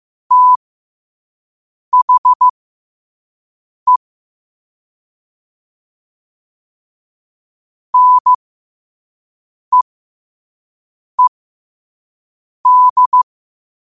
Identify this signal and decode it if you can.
Morse code, heard as a keyed tone, about 15 wpm, spaced as for 5 wpm, 1000 Hz. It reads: THE NEED
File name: qotdfiles / 5wpm